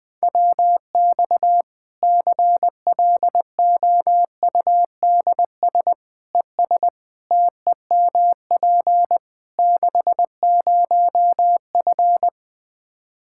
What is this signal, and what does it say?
Morse code, heard as a keyed tone, about 20 wpm, 700 Hz. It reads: WX CLOUDS ES TEMP 60F
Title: Practice QSO - Exchange 2 - Info - 20wpm